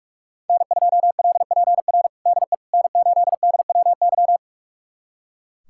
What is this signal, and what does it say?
Morse code, heard as a keyed tone, about 41 wpm, 700 Hz. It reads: N2LPR DE N8DWY
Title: USA Callsign DE Callsign - 5 Characters - 45wpm